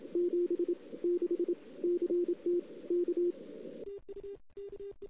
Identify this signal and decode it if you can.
Morse code, heard as a keyed tone, about 27 wpm, spaced as for 22 wpm, 350 Hz. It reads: 76 CT K
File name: CW / 2008-ss-cw-k5tr-op-n5rz -01:00